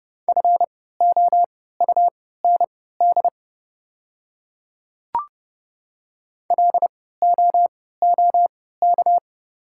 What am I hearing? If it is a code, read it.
Morse code, heard as a keyed tone, about 30 wpm, spaced as for 17 wpm, 690 Hz. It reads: FOUND LOOK
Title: 3x Character Spacing - Top 300 Words - 30wpm